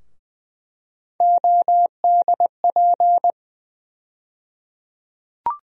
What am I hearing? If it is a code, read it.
Morse code, heard as a keyed tone, about 20 wpm, 700 Hz. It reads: ODP
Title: USA Call Sign Postfixes - 3 Letters 20wpm